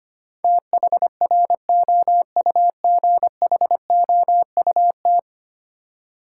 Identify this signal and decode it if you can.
Morse code, heard as a keyed tone, about 25 wpm, 700 Hz. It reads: THROUGHOUT
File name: Ten Letter Words - 25wpm